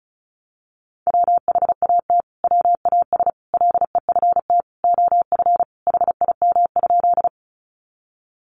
Morse code, heard as a keyed tone, about 34 wpm, 700 Hz. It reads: WHAT WAS LEFT OF HIM?
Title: Sentences from Top 300 Words 35wpm